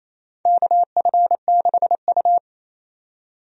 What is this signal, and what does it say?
Morse code, heard as a keyed tone, about 28 wpm, 700 Hz. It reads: KF6U
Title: USA Call Signs - 4 Characters - 28wpm